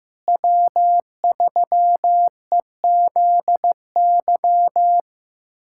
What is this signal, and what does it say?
Morse code, heard as a keyed tone, about 15 wpm, 700 Hz. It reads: W3EZY
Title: USA Callsign DE Callsign - 5 Characters - 15wpm